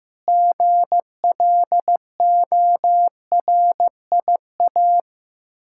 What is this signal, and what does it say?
Morse code, heard as a keyed tone, about 15 wpm, 700 Hz. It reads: GLORIA
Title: K1USN SST QSO Practice 15wpm